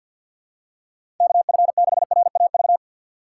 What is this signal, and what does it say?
Morse code, heard as a keyed tone, about 50 wpm, 700 Hz. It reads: KF6RAV
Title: USA Call Signs - 6 Characters - 6x Character Spacing - 50wpm